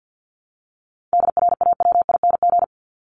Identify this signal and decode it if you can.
Morse code, heard as a keyed tone, about 52 wpm, 700 Hz. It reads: DRAWING
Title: ING Suffix - Top 100 Words with Base Words 50wpm